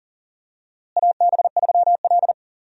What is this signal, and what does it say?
Morse code, heard as a keyed tone, about 40 wpm, 700 Hz. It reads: AB3L
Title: USA Call Signs - 4 Characters - Repeated 2x - 40wpm